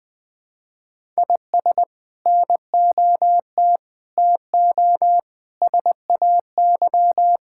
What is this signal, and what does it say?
Morse code, heard as a keyed tone, about 20 wpm, 700 Hz. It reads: IS NOT TO SAY
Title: Build A Phrase - 5-Word Phrases - 20wpm